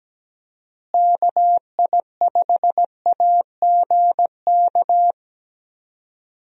Morse code, heard as a keyed tone, about 17 wpm, 700 Hz. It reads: KI5AGK
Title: USA Call Signs - 6 Characters - Repeated 2x - 17wpm